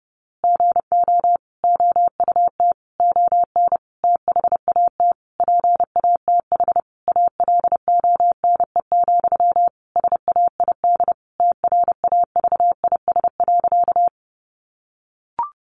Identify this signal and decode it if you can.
Morse code, heard as a keyed tone, about 30 wpm, 700 Hz. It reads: GO OUT ON THAT PATH ALONE, SAID TRAVIS.
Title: A Sound Of Thunder 30wpm